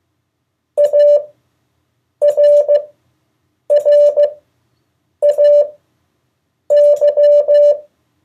Morse code, heard as a keyed tone, about 15 wpm, spaced as for 7 wpm, 580 Hz. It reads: ARRAY
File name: 2of4-words-1